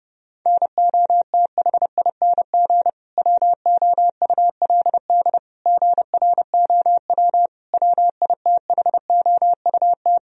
Morse code, heard as a keyed tone, about 31 wpm, 700 Hz. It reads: NOTHING WOULD GROW WITHOUT
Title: Sentences from Top 1000 Words - Review Part 3 - 30wpm